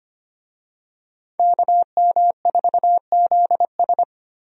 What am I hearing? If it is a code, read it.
Morse code, heard as a keyed tone, about 25 wpm, 700 Hz. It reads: KM4ZS
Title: USA Call Signs - 5 Characters - Repeated 2x - 25wpm